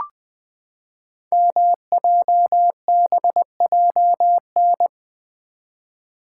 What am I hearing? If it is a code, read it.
Morse code, heard as a keyed tone, about 20 wpm, 700 Hz. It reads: MJBJN
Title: Code Groups - 5 Letters - 20wpm